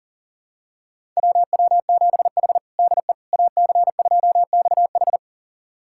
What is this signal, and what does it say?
Morse code, heard as a keyed tone, about 40 wpm, 700 Hz. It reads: WW7H DE AC2XH
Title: USA Callsign DE Callsign 40wpm